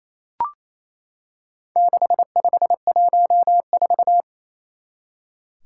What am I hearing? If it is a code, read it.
Morse code, heard as a keyed tone, about 29 wpm, 700 Hz. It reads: 6514
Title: Four Challenging Letter-Number Combinations 28wpm